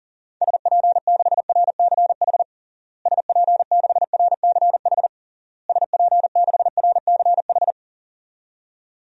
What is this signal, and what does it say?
Morse code, heard as a keyed tone, about 40 wpm, 700 Hz. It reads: SP6RCH SP6RCH SP6RCH
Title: DXCC-100 Call Signs - 6 Characters - Repeated 3x - 40wpm